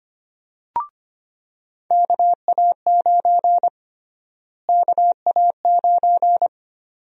Morse code, heard as a keyed tone, about 25 wpm, 700 Hz. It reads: KA9 KA9
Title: USA Call Sign Prefixes - Repeated 2x - 25wpm